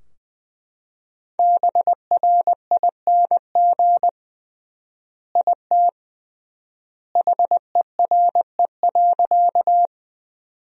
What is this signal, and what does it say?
Morse code, heard as a keyed tone, about 21 wpm, 700 Hz. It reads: BRING IT HERE.
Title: 3x Word-Spacing - Sentences from Top 500 Words - 20wpm